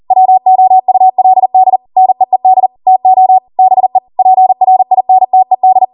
Morse code, heard as a keyed tone, about 40 wpm, 750 Hz. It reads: WOULD NEED TO BE PRINTED